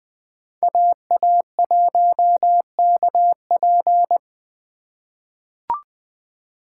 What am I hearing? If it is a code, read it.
Morse code, heard as a keyed tone, about 20 wpm, 700 Hz. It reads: AA1KP
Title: USA Call Signs - 5 Characters - 4x Character Spacing - 20wpm